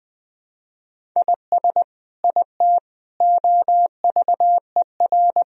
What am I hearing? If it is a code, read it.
Morse code, heard as a keyed tone, about 21 wpm, 700 Hz. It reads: IS IT OVER
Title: Sentences from Top 1000 Words - Review Part 1 - 20wpm